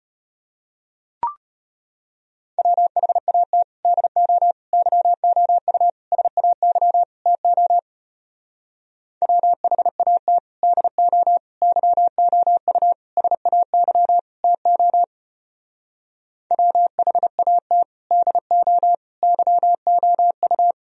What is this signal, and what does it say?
Morse code, heard as a keyed tone, about 30 wpm, 700 Hz. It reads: WHAT DO YOU SAY TO WHAT DO YOU SAY TO WHAT DO YOU
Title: Speed-Racing - Sets of 5 Words 25wpm